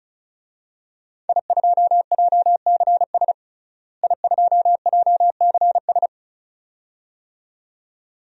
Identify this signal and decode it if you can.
Morse code, heard as a keyed tone, about 36 wpm, 700 Hz. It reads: I2JCS I2JCS
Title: DXCC-100 Call Signs - 5 Characters - Repeated 2x - 35wpm